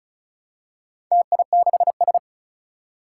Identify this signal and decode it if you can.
Morse code, heard as a keyed tone, about 36 wpm, 700 Hz. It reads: TI6S
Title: Four Challenging Letter-Number Combinations 35wpm